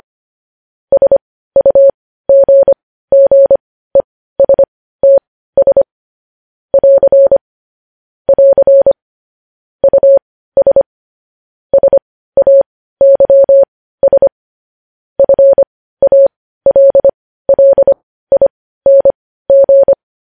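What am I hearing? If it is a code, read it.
Morse code, heard as a keyed tone, about 25 wpm, spaced as for 15 wpm, 550 Hz. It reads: SUGGESTS <AR> <AR> US SAYS FALLING
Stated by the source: Morse code transcription of news headlines.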